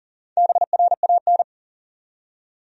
Morse code, heard as a keyed tone, about 40 wpm, 700 Hz. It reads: BRAN
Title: USA Common Names - Max 4 Letters - 40wpm